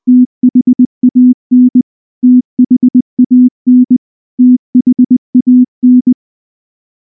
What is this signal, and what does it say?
Morse code, heard as a keyed tone, about 20 wpm, 250 Hz. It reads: THAN THAN THAN
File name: morse / morse-training-20WPM-5